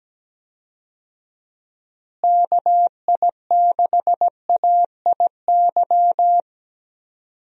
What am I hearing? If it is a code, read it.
Morse code, heard as a keyed tone, about 17 wpm, 700 Hz. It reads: KI6AIY
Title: USA Call Signs - 6 Characters - Repeated 3x - 17wpm